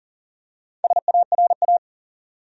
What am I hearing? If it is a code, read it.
Morse code, heard as a keyed tone, about 40 wpm, 700 Hz. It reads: SARA
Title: USA Common Names - Max 5 Letters - 40wpm